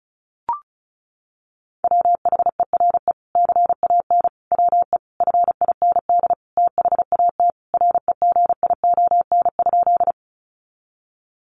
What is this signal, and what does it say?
Morse code, heard as a keyed tone, about 34 wpm, 700 Hz. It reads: WHERE CAN WE FIND THAT REGION?
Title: Sentences from Top 1500 Words 35wpm